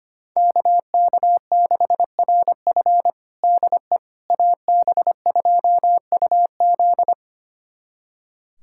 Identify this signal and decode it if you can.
Morse code, heard as a keyed tone, about 25 wpm, 700 Hz. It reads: KK6RF DE AB2UZ
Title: USA Callsign DE Callsign - 5 Characters - 25wpm